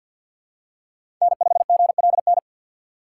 Morse code, heard as a keyed tone, about 53 wpm, 700 Hz. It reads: N5GLN
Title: USA Call Signs - 5 Characters - 8x Character Spacing - 50wpm